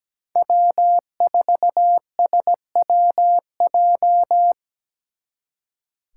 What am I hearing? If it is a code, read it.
Morse code, heard as a keyed tone, about 20 wpm, 700 Hz. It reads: W4SWJ
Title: Speed Racing - USA 1x3 Call Signs 17wpm